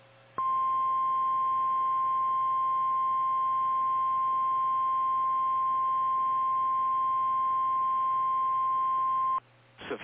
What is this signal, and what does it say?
Emergency Alert System (SAME) tones (the 1050 Hz attention tone). It is heard from 0.3 to 9.5 s.
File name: eas / KKPRKXPN / KKPRKXPN